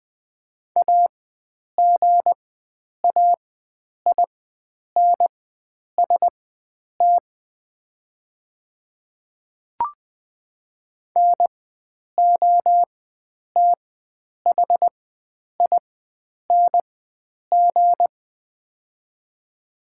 Morse code, heard as a keyed tone, about 20 wpm, spaced as for 9 wpm, 700 Hz. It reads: AGAINST NOTHING
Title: 4x Character Spacing - Top 500 Words - 20wpm